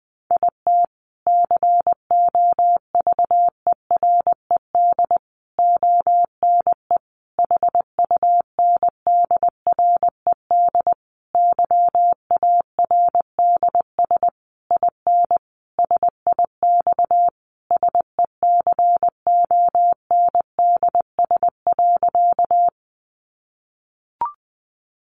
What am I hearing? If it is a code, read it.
Morse code, heard as a keyed tone, about 20 wpm, 700 Hz. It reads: IT COVERED ONE HUNDRED YARDS IN SIX SECONDS.
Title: A Sound Of Thunder 20wpm